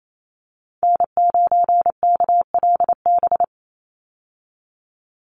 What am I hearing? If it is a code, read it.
Morse code, heard as a keyed tone, about 28 wpm, 700 Hz. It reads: N9KLB
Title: USA Call Signs 28wpm